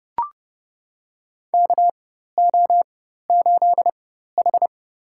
Morse code, heard as a keyed tone, about 30 wpm, spaced as for 14 wpm, 700 Hz. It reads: KO8H
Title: USA Call Signs - 4x Character Spacing - 30wpm